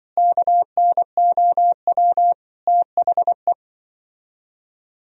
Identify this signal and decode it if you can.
Morse code, heard as a keyed tone, about 25 wpm, 700 Hz. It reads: KNOW THE